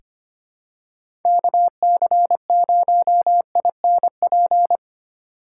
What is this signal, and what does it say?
Morse code, heard as a keyed tone, about 25 wpm, 700 Hz. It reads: KC0INP
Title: K1USN SST QSO Practice - 25wpm